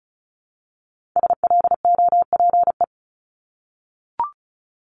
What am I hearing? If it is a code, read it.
Morse code, heard as a keyed tone, about 35 wpm, 700 Hz. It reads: SLOPE
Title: Multi-letter Phonemes Repeated 3x 35wpm